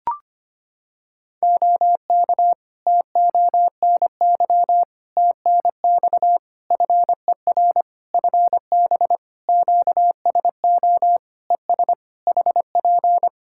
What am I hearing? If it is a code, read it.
Morse code, heard as a keyed tone, about 25 wpm, 700 Hz. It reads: OK TONY TNX FER FB QSO ES HP
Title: Practice QSO - Exchange 4 - Ending - 25wpm